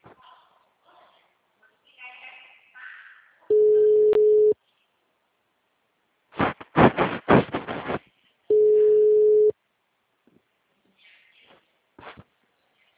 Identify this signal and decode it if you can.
Ringback tone (Europe, 425 Hz), with speech.
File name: out- -2038- - - .27276